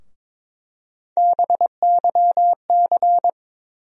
Morse code, heard as a keyed tone, about 22 wpm, 700 Hz. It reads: BYC
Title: USA Call Sign Postfixes - 3 Letters 22wpm